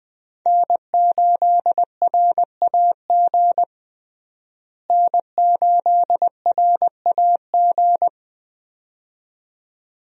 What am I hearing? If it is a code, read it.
Morse code, heard as a keyed tone, about 20 wpm, 700 Hz. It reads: N8RAG N8RAG
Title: USA Call Signs - 5 Characters - Repeated 2x - 20wpm